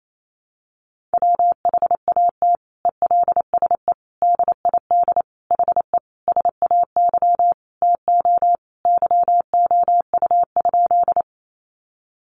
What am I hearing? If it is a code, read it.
Morse code, heard as a keyed tone, about 28 wpm, 700 Hz. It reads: WHAT ELSE DID HE SAY TO YOU?
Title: Sentences from Top 800 Words 28wpm